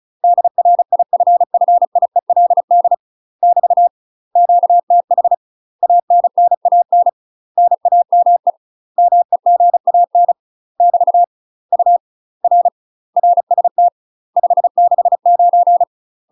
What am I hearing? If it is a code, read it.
Morse code, heard as a keyed tone, about 34 wpm, 700 Hz. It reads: DRIFFIELD <BT> QTH ANNAN NAME MEGAN <BT> U R RST 569
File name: UK MASTER QSOs TRAINING / 35WPM / UK MASTER QSOs 0037